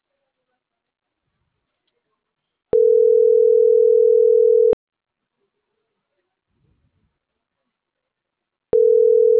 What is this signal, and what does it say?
Ringback tone (North America), 440 + 480 Hz.